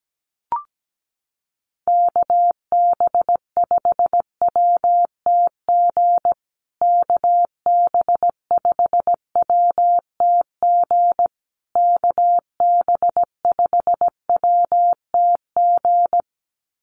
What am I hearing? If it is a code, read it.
Morse code, heard as a keyed tone, about 17 wpm, 700 Hz. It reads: KB5WTG KB5WTG KB5WTG
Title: USA Call Signs Repeated 3x 17wpm